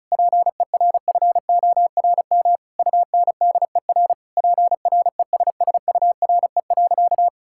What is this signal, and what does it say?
Morse code, heard as a keyed tone, about 36 wpm, 700 Hz. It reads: PERFORM UNDER PRESSURE.
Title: Sentences from Top 1000 Words - Review Part 6 - 35wpm